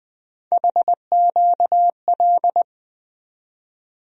Morse code, heard as a keyed tone, about 20 wpm, 700 Hz. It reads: HQL
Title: USA Call Sign Postfixes - 3 Letters 20wpm